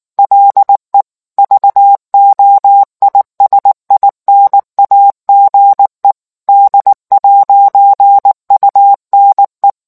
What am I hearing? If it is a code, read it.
Morse code, heard as a keyed tone, about 19 wpm, 800 Hz. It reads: LE VOISINAGE D'UNE